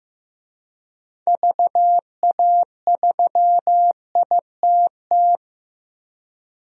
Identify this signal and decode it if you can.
Morse code, heard as a keyed tone, about 15 wpm, 700 Hz. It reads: VA3ITT
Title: DXCC-100 Call Signs - 6 Characters - 15wpm